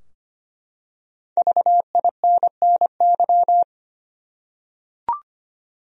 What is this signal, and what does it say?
Morse code, heard as a keyed tone, about 25 wpm, 700 Hz. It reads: VINNY
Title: USA Common Names - 25wpm